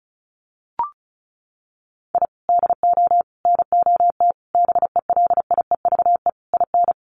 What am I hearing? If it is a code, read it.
Morse code, heard as a keyed tone, about 34 wpm, 700 Hz. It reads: I DO NOT BELIEVE IN
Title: Sets of 5 Words Encore 35wpm